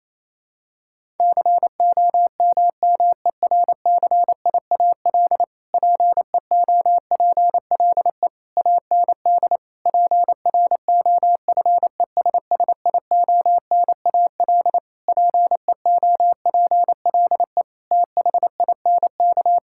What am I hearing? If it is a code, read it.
Morse code, heard as a keyed tone, about 28 wpm, 700 Hz. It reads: COMMERCIAL PEOPLE AND PROFESSIONAL PEOPLE THINK
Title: Sentences from Top 1000 Words - Review Part 5 - 28wpm